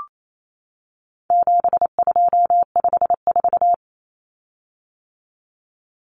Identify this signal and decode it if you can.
Morse code, heard as a keyed tone, about 28 wpm, 700 Hz. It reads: 7254
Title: Common Cut Numbers - 4-Digit 28wpm